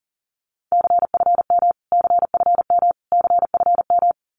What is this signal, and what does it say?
Morse code, heard as a keyed tone, about 40 wpm, 700 Hz. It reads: CFM CFM CFM
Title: Abbreviations Repeated 3x - 40wpm